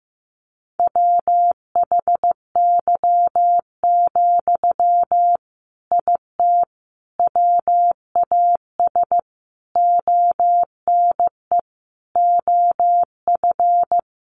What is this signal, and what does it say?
Morse code, heard as a keyed tone, about 15 wpm, 700 Hz. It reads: WHY, IT WAS ONE OF
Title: Sentences from Top 300 Words 15wpm